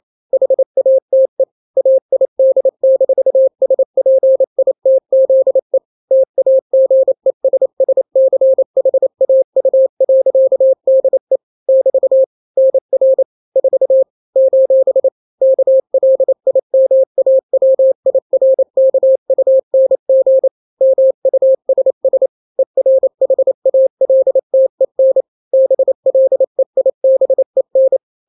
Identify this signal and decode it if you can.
Morse code, heard as a keyed tone, about 28 wpm, 530 Hz. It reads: HATE AID-SPITZE TAGESSCHAU.DE <BT> NR 4 : KLIMAWIRKUNG MUSS ERHALTEN BLEIBEN
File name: daily / 27wpm